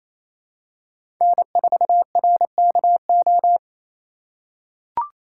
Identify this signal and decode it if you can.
Morse code, heard as a keyed tone, about 28 wpm, 700 Hz. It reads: N4RKO
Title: USA Call Signs - 5 Characters - 2x Character Spacing - 28wpm